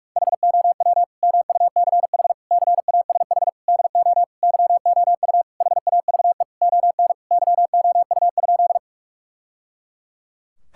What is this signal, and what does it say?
Morse code, heard as a keyed tone, about 46 wpm, 700 Hz. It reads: HOW MUCH CASH DO YOU HAVE ON YOU?
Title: Sentences from Top 1000 Words - Review Part 5 - 45wpm